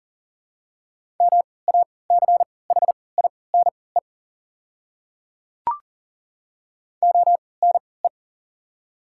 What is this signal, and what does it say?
Morse code, heard as a keyed tone, about 40 wpm, spaced as for 23 wpm, 690 Hz. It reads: MACHINE ONE
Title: 3x Character Spacing - Top 500 Words - 40wpm